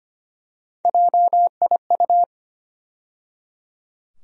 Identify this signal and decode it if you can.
Morse code, heard as a keyed tone, about 25 wpm, 700 Hz. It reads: JIU
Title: USA Call Sign Postfixes - 3 Letters 25wpm